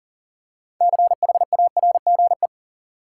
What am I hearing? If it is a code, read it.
Morse code, heard as a keyed tone, about 40 wpm, 700 Hz. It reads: CHARGE
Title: Six Letter Words - 40wpm